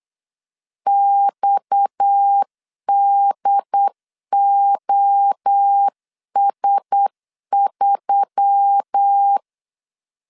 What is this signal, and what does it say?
Morse code, heard as a keyed tone, about 8 wpm, 800 Hz. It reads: XDOS3
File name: LEZIONI / 7 GF9Z / 7 G 8wpm 3118